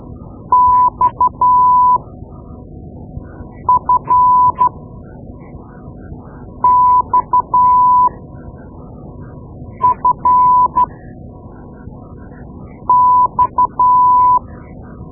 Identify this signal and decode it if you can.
Morse code, heard as a keyed tone, about 12 wpm, spaced as for 4 wpm, 1000 Hz. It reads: XFXFX
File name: Morse Lessons / Morse Lessons / Tape-2B